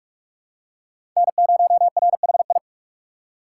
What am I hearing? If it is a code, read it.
Morse code, heard as a keyed tone, about 46 wpm, 700 Hz. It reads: N0RHI
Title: USA Call Signs - 5 Characters - 4x Character Spacing - 45wpm